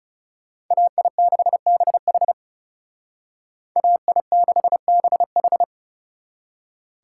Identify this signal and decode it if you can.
Morse code, heard as a keyed tone, about 39 wpm, 700 Hz. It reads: AI6BH AI6BH